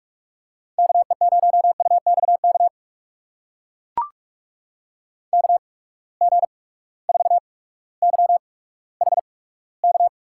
Morse code, heard as a keyed tone, about 43 wpm, spaced as for 19 wpm, 700 Hz. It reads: KE0UXK KG4YHK
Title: USA Call Signs - 6 Characters - 8x Character Spacing - 45wpm